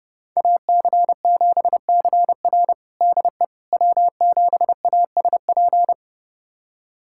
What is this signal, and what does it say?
Morse code, heard as a keyed tone, about 30 wpm, 700 Hz. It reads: AC7CR DE W7ASP
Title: USA Callsign DE Callsign - 5 Characters - 30wpm